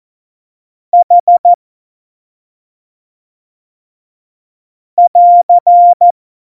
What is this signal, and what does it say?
Morse code, heard as a keyed tone, about 14 wpm, 700 Hz. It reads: H <AR>
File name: Morsekurs / Witze / Witz 16. Januar 2018, 25 F70